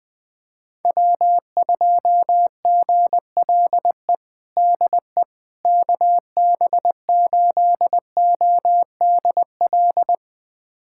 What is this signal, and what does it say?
Morse code, heard as a keyed tone, about 20 wpm, 700 Hz. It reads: W2GLE DE KB8ODL
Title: USA Callsign DE Callsign 20wpm